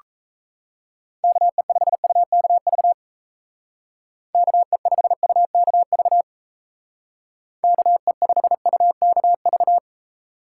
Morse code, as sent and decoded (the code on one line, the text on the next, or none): -.- . ..... ..- -.- ...- / -.- . ..... ..- -.- ...- / -.- . ..... ..- -.- ...-
KE5UKV KE5UKV KE5UKV